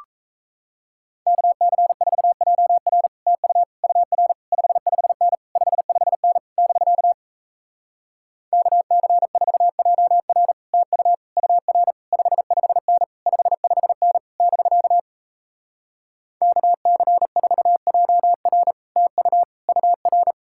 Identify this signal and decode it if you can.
Morse code, heard as a keyed tone, about 37 wpm, 700 Hz. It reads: KC4JR TU UR 55N 55N <BK> KC4JR TU UR 55N 55N <BK> KC4JR TU UR